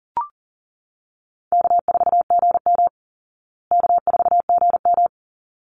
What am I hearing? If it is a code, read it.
Morse code, heard as a keyed tone, about 40 wpm, 700 Hz. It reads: K4GM K4GM
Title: Easy Call Signs Repeated 2x - 40wpm